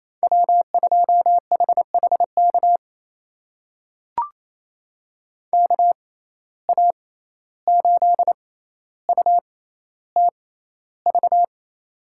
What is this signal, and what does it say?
Morse code, heard as a keyed tone, about 28 wpm, spaced as for 14 wpm, 700 Hz. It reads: W2HHK KA8UTV
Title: USA Call Signs - 6x Character Spacing - 28wpm